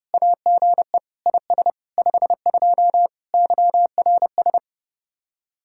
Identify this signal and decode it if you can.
Morse code, heard as a keyed tone, about 36 wpm, 700 Hz. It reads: AGE IS 52 YRS
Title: Speed-Racing Common QSO Phrases Encore - 30wpm